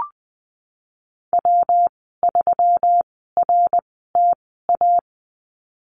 Morse code, heard as a keyed tone, about 20 wpm, spaced as for 15 wpm, 700 Hz. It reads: W3RTA
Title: 238 Lesson 38g - Letter q - Call Signs 20wpm